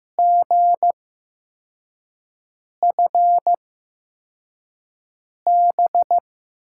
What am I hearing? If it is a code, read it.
Morse code, heard as a keyed tone, about 15 wpm, spaced as for 4 wpm, 700 Hz. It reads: GFB
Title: USA Call Signs - 6 Characters - 8x Character Spacing - 15wpm